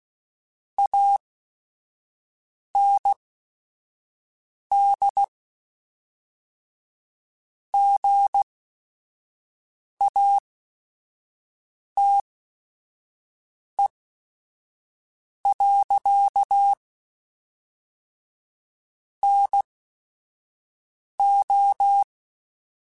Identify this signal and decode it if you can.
Morse code, heard as a keyed tone, about 16 wpm, spaced as for 5 wpm, 800 Hz. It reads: AND GATE. NO
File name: Technical Subjects / Technical Subjects-mp3s / AND Gate @5